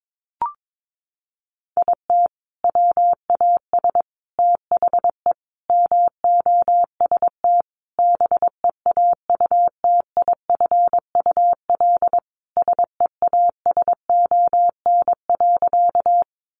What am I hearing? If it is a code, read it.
Morse code, heard as a keyed tone, about 22 wpm, 700 Hz. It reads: IT WAS THE MOST BEAUTIFUL SEASON.
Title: Sentences from Top 1400 Words 22wpm